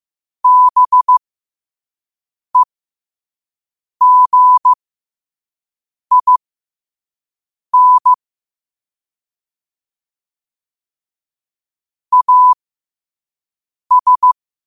Morse code, heard as a keyed tone, about 15 wpm, spaced as for 5 wpm, 1000 Hz. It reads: BEGIN AS
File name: qotdfiles / 5wpm